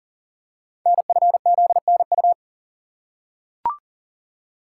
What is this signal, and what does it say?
Morse code, heard as a keyed tone, about 38 wpm, 700 Hz. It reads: NFZNU
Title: Code Groups - 5 Letters - 40wpm